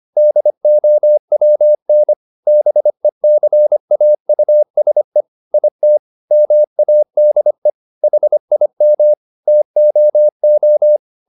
Morse code, heard as a keyed tone, about 25 wpm, 600 Hz. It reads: DOWN BECAUSE IT MADE HIM TOO
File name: the old man and the sea part 1 25wpm